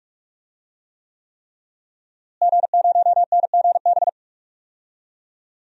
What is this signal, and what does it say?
Morse code, heard as a keyed tone, about 45 wpm, 700 Hz. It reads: G0NGB
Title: DXCC-100 Call Signs - 5 Characters - Repeated 3x - 45wpm